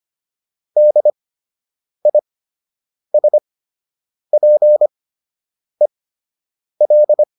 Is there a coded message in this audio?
Yes: Morse code from 0.8 to 7.2 s, heard as a keyed tone, about 25 wpm, spaced as for 8 wpm, 600 Hz. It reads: DISPEL